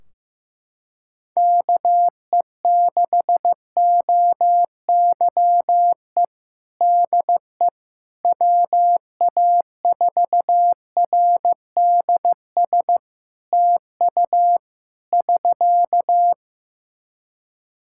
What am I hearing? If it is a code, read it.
Morse code, heard as a keyed tone, about 15 wpm, 700 Hz. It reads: KE6OYE DE WA4RDS TU <SK>
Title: Practice QSO - Exchange 4 - Ending - 15wpm